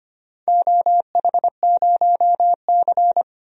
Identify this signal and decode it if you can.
Morse code, heard as a keyed tone, about 25 wpm, 700 Hz. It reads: OH0C
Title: DXCC-100 Call Signs - 4 Characters - 25wpm